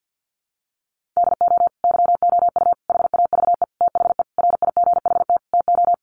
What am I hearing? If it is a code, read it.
Morse code, heard as a keyed tone, about 50 wpm, 700 Hz. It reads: DO YOU HAVE THE RIGHT TO
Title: Sentences from Top 1400 Words 50wpm